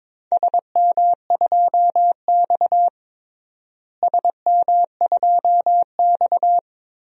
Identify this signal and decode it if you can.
Morse code, heard as a keyed tone, about 22 wpm, 700 Hz. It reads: SM2X SM2X
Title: DXCC-100 Call Signs - 4 Characters - Repeated 3x - 22wpm